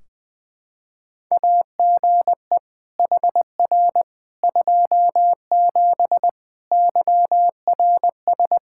Morse code, heard as a keyed tone, about 21 wpm, 700 Hz. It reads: AGE HR 27 YRS
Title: Practice QSO - Exchange 3 - More Info - 20wpm